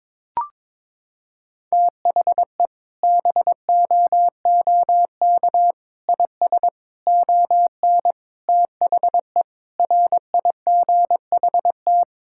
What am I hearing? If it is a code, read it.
Morse code, heard as a keyed tone, about 22 wpm, 700 Hz. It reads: THE BOOK IS ON THE RIGHT
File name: Sentences with 7-Word Phrases - 22wpm